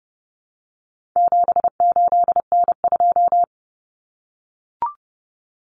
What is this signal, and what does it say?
Morse code, heard as a keyed tone, about 30 wpm, 700 Hz. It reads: 78N2
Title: Common Cut Numbers - 4-Digit 30wpm